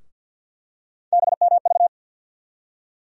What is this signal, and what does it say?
Morse code, heard as a keyed tone, about 51 wpm, 700 Hz. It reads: BMV
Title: USA Call Sign Postfixes - 3 Letters 50wpm